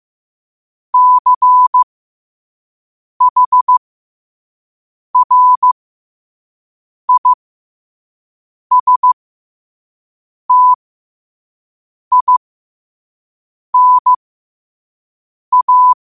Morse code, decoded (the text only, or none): CHRISTINA